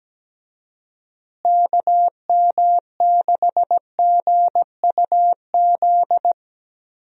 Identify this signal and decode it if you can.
Morse code, heard as a keyed tone, about 17 wpm, 700 Hz. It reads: KM6GUZ
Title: USA Call Signs - 6 Characters - Repeated 2x - 17wpm